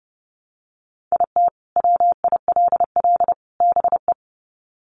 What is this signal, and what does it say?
Morse code, heard as a keyed tone, about 30 wpm, 700 Hz. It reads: IT WILL BE
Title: Sets of 3 Words 30wpm